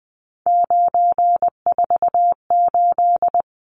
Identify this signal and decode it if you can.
Morse code, heard as a keyed tone, about 20 wpm, 700 Hz. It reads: 948
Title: Numbers - 3-Digit 20wpm